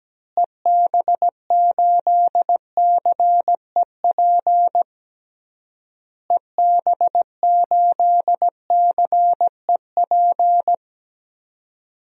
Morse code, heard as a keyed tone, about 17 wpm, 700 Hz. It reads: EB8CEP EB8CEP
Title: DXCC-100 Call Signs - 6 Characters - Repeated 3x - 17wpm